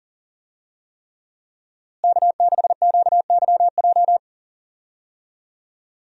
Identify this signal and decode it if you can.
Morse code, heard as a keyed tone, about 37 wpm, 700 Hz. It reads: K6QYJ
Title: USA Call Signs - 5 Characters - Repeated 3x - 40wpm